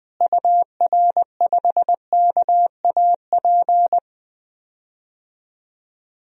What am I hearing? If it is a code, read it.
Morse code, heard as a keyed tone, about 21 wpm, 700 Hz. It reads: UR5KAP
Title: DXCC-100 Call Signs - 6 Characters - Repeated 3x - 20wpm